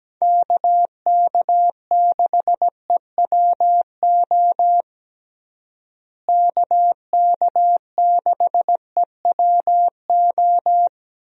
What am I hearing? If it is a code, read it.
Morse code, heard as a keyed tone, about 17 wpm, 700 Hz. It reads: KK6EWO KK6EWO
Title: USA Call Signs - 6 Characters - Repeated 2x - 17wpm